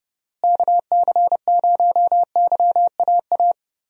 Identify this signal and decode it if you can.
Morse code, heard as a keyed tone, about 32 wpm, 700 Hz. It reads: KC0YAA
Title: USA Call Signs - 6 Characters - Repeated 3x - 30wpm